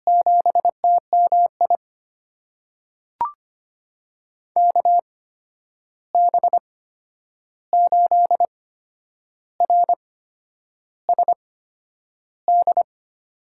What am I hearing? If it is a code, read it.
Morse code, heard as a keyed tone, about 25 wpm, spaced as for 9 wpm, 700 Hz. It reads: 7TMI KB8RSD
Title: USA Call Signs - 6 Characters - 8x Character Spacing - 25wpm